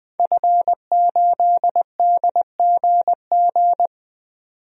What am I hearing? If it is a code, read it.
Morse code, heard as a keyed tone, about 20 wpm, 700 Hz. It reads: F8DGG
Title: DXCC-100 Call Signs - 5 Characters - 20wpm